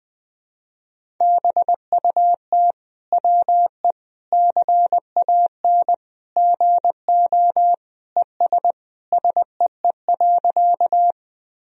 Morse code, heard as a keyed tone, about 21 wpm, 700 Hz. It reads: BUT WE CAN GO ES SEE.
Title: Sentences from Top 100 Words and Abbreviations 20wpm